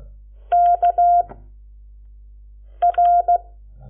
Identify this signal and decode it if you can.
Morse code, heard as a keyed tone, about 16 wpm, 700 Hz. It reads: K R
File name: Lecciones / CLASE Nº7